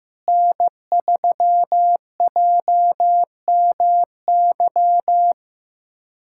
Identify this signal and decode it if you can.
Morse code, heard as a keyed tone, about 15 wpm, 700 Hz. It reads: N3JMY
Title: USA Callsign DE Callsign 15wpm